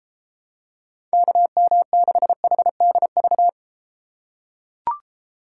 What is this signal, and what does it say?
Morse code, heard as a keyed tone, about 32 wpm, 700 Hz. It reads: KM6HDV